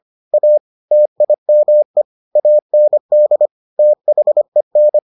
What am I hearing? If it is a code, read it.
Morse code, heard as a keyed tone, about 25 wpm, 600 Hz. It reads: A TIME AND THEN